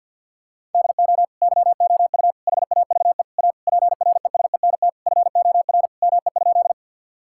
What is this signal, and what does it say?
Morse code, heard as a keyed tone, about 50 wpm, 700 Hz. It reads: DO YOU HAVE A PRESENT FOR ME?
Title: Sentences from Top 1000 Words - Review Part 5 - 50wpm